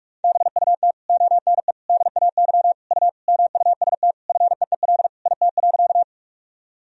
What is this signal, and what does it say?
Morse code, heard as a keyed tone, about 46 wpm, 700 Hz. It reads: BUT ONE DAY U MUST FEEL IT.
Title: Sentences from Top 400 Words and Abbreviations 45wpm